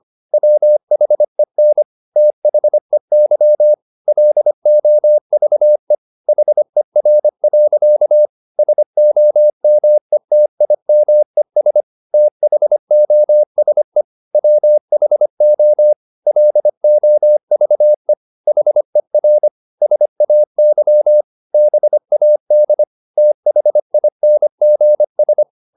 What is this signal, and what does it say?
Morse code, heard as a keyed tone, about 25 wpm, 600 Hz. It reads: WHEN THEY LOVE HER. SOMETIMES THOSE WHO LOVE HER SAY BAD THINGS
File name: the old man and the sea part 1 25wpm